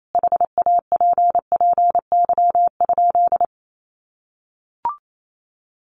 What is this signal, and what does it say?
Morse code, heard as a keyed tone, about 28 wpm, 700 Hz. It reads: HAPPY?
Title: Sentences from Top 1000 Words 28wpm